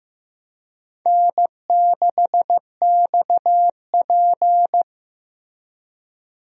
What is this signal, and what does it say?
Morse code, heard as a keyed tone, about 15 wpm, 700 Hz. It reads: N6XP
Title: USA Call Signs - 4 Characters - Repeated 2x - 15wpm